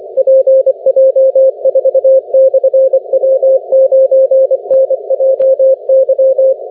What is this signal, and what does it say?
Morse code, heard as a keyed tone, about 24 wpm, 530 Hz. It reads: PJ4/W9NJY
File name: wp / pj4 w9njy 17m cw 28 mar 2011 2054z